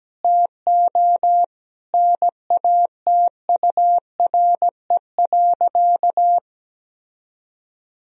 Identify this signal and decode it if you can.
Morse code, heard as a keyed tone, about 17 wpm, 700 Hz. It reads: TO NATURE.
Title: Sentences from Top 1000 Words - Review Part 4 - 17wpm